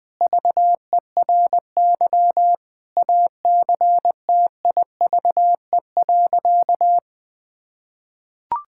Morse code, heard as a keyed tone, about 20 wpm, 700 Hz. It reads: VERY ACTIVE.
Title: Sentences from Top 1000 Words - Review Part 6 - 20wpm